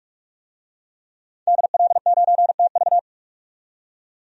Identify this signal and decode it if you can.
Morse code, heard as a keyed tone, about 43 wpm, 700 Hz. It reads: DL9TV
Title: DXCC-100 Call Signs - 5 Characters - 45wpm